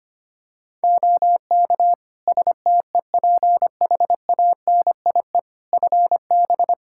Morse code, heard as a keyed tone, about 25 wpm, 700 Hz. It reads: OK STEPHANIE FB
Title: Practice QSO - Exchange 2 - Info - 25wpm